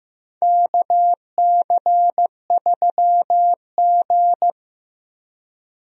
Morse code, heard as a keyed tone, about 15 wpm, 700 Hz. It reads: KC3G
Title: USA Call Signs - 4 Characters - Repeated 3x - 15wpm